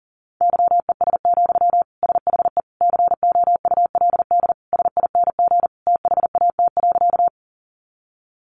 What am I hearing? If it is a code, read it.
Morse code, heard as a keyed tone, about 40 wpm, 700 Hz. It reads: YES, SHE COULD SING THAT.
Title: Sentences from Top 500 Words 40wpm